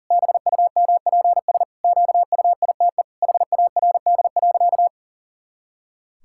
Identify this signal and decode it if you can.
Morse code, heard as a keyed tone, about 40 wpm, 700 Hz. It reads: BUMPS QUITE HARD.
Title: Pangram Challenge v2 - 40wpm